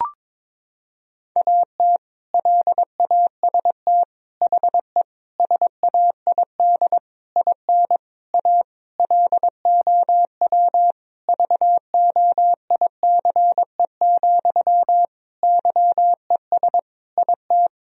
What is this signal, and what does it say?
Morse code, heard as a keyed tone, about 22 wpm, 700 Hz. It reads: AT LAST HE SAID IN A LOW VOICE, YES IT
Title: Sentences from Top 1000 Words - Review Part 1 - 22wpm